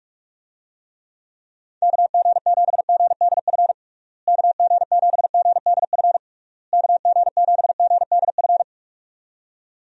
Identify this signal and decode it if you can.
Morse code, heard as a keyed tone, about 47 wpm, 700 Hz. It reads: KG7GDF KG7GDF KG7GDF
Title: USA Call Signs - 6 Characters - Repeated 3x - 45wpm